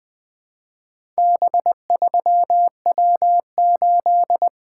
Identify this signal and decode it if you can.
Morse code, heard as a keyed tone, about 20 wpm, 700 Hz. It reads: B3W8
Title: Four Letter-Number Combinations - 20wpm